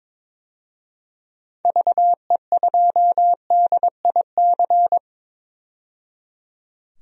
Morse code, heard as a keyed tone, about 22 wpm, 700 Hz. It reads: VE2DIC
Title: DXCC-100 Call Signs - 6 Characters - 22wpm